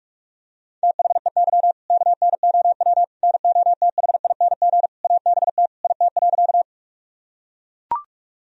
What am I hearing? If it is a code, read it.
Morse code, heard as a keyed tone, about 41 wpm, 690 Hz. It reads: THEY KNOW NOTHING ABT IT.
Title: Sentences from Top 500 Words and Abbreviations 45wpm